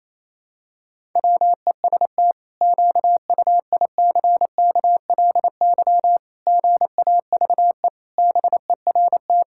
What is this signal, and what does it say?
Morse code, heard as a keyed tone, about 28 wpm, 700 Hz. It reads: WEST QUICKLY GAVE BERT
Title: Pangram Challenge v2 - 28wpm